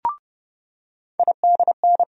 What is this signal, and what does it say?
Morse code, heard as a keyed tone, about 30 wpm, 700 Hz. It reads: IDN
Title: USA Call Sign Postfixes - 3 Letters 30wpm